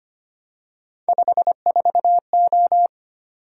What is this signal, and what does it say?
Morse code, heard as a keyed tone, about 25 wpm, 700 Hz. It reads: 54O
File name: Three Letter-Number Combinations - 25wpm